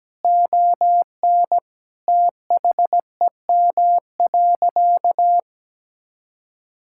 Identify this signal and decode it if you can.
Morse code, heard as a keyed tone, about 17 wpm, 700 Hz. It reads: ON THEM.
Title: Sentences from Top 1600 Words 17wpm